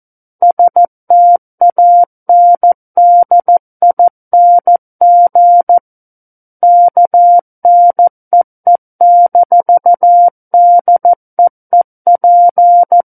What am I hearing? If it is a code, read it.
Morse code, heard as a keyed tone, about 14 wpm, 700 Hz. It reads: STANDING KNEE-DEEP
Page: War of the Worlds - 05-Chapter 5 - 14 WPM